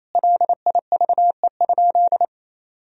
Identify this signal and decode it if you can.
Morse code, heard as a keyed tone, about 28 wpm, 700 Hz. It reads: LIVE?
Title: Sentences from Top 1000 Words - Review Part 1 - 28wpm